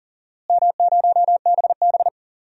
Morse code, heard as a keyed tone, about 37 wpm, 700 Hz. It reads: M0BB
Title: DXCC-100 Call Signs - 4 Characters - Repeated 3x - 40wpm